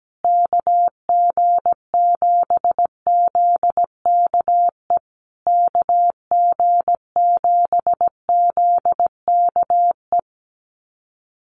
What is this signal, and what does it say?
Morse code, heard as a keyed tone, about 17 wpm, 700 Hz. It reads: KG7ZKE KG7ZKE
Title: USA Call Signs Repeated 3x 17wpm